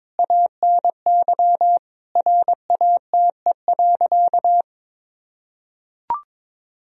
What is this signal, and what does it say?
Morse code, heard as a keyed tone, about 21 wpm, 700 Hz. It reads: ANY RATE.
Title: Sentences from Top 1000 Words - Review Part 4 - 22wpm